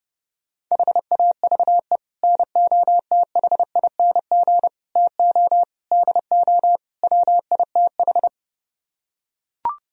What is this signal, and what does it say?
Morse code, heard as a keyed tone, about 31 wpm, 700 Hz. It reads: HAVE NOTHING TO DO WITH